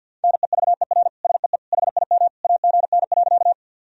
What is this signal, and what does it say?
Morse code, heard as a keyed tone, about 52 wpm, 700 Hz. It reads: NEVER SEE HIM AGN.
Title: Sentences from Top 300 Words and Abbreviations 50wpm